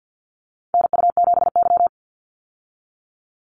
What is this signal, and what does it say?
Morse code, heard as a keyed tone, about 50 wpm, 700 Hz. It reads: NU7Y
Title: Call Signs - 50wpm